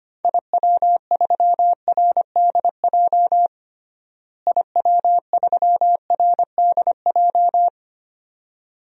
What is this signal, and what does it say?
Morse code, heard as a keyed tone, about 25 wpm, 700 Hz. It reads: IW3RDJ IW3RDJ
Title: DXCC-100 Call Signs - 6 Characters - Repeated 2x - 25wpm